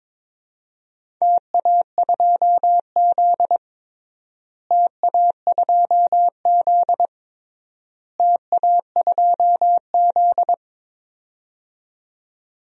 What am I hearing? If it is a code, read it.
Morse code, heard as a keyed tone, about 21 wpm, 700 Hz. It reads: TA2Z TA2Z TA2Z
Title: DXCC-100 Call Signs - 4 Characters - Repeated 3x - 22wpm